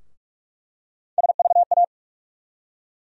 Morse code, heard as a keyed tone, about 44 wpm, 700 Hz. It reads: SVA
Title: USA Call Sign Postfixes - 3 Letters 45wpm